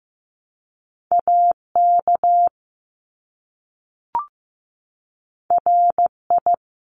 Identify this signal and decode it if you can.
Morse code, heard as a keyed tone, about 15 wpm, 700 Hz. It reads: AK RI
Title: ARRL Sections 15wpm